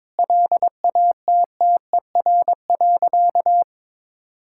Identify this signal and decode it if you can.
Morse code, heard as a keyed tone, about 22 wpm, 700 Hz. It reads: LATTER.
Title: Sentences from Top 1600 Words 22wpm